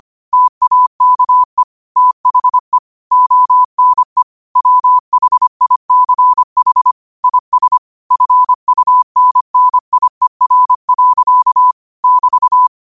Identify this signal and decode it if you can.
Morse code, heard as a keyed tone, about 25 wpm, 1000 Hz. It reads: TAKE THE ONE WHICH IS FUNNIER. <BT>